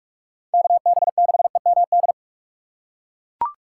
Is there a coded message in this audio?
Yes: Morse code from 0.5 to 2.1 s, heard as a keyed tone, about 41 wpm, 700 Hz. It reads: KB6EMD